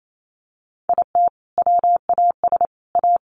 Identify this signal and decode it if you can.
Morse code, heard as a keyed tone, about 28 wpm, 700 Hz. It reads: IT WAS A
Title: Sentences from Top 1300 Words 28wpm